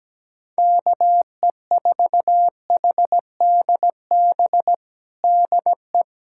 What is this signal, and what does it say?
Morse code, heard as a keyed tone, about 17 wpm, 700 Hz. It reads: KE4HDB DE
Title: USA Callsign DE Callsign - 6 Characters - 17wpm